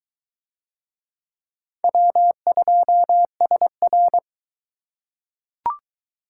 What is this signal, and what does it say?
Morse code, heard as a keyed tone, about 22 wpm, 700 Hz. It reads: W2SR